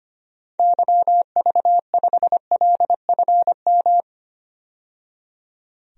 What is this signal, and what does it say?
Morse code, heard as a keyed tone, about 25 wpm, 700 Hz. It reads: YV5LFM
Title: DXCC-100 Call Signs - 6 Characters - 25wpm